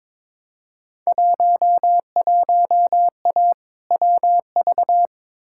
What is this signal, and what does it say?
Morse code, heard as a keyed tone, about 22 wpm, 700 Hz. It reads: 11A WV
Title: Field Day Exchanges Encore - 22wpm